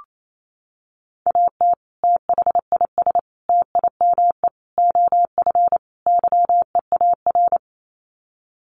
Morse code, heard as a keyed tone, about 28 wpm, 700 Hz. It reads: AT THIS TIME OF YEAR
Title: Sets of 5 Words Encore2 28wpm